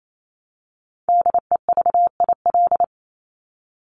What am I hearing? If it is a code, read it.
Morse code, heard as a keyed tone, about 28 wpm, 700 Hz. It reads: DEVIL
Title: Top 1751-2000 Words 28wpm